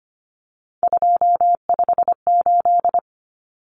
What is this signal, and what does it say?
Morse code, heard as a keyed tone, about 25 wpm, 700 Hz. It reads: 258
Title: Numbers - 3-Digit 25wpm